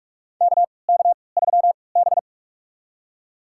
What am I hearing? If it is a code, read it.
Morse code, heard as a keyed tone, about 44 wpm, spaced as for 25 wpm, 690 Hz. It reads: KK3B
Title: USA Call Signs - 4 Characters - 3x Character Spacing - 45wpm